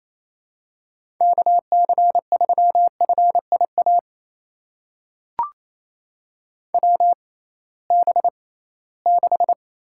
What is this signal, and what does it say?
Morse code, heard as a keyed tone, about 28 wpm, spaced as for 18 wpm, 700 Hz. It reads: KC3FIA WB6
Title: USA Call Signs - 6 Characters - 6x Character Spacing - 28wpm